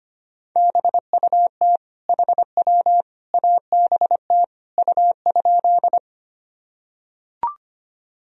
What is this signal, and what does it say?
Morse code, heard as a keyed tone, about 25 wpm, 700 Hz. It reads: BUT HW ABT U?
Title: Sentences from Top 100 Words and Abbreviations 25wpm